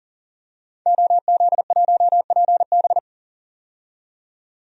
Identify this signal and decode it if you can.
Morse code, heard as a keyed tone, about 40 wpm, 700 Hz. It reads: OZ1PB
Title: DXCC-100 Call Signs - 5 Characters - 40wpm